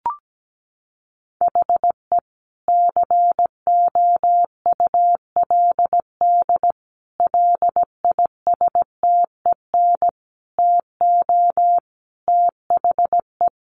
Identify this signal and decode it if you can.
Morse code, heard as a keyed tone, about 17 wpm, 700 Hz. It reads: HE COULD LISTEN TO THE
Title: Sentences from Top 1500 Words 17wpm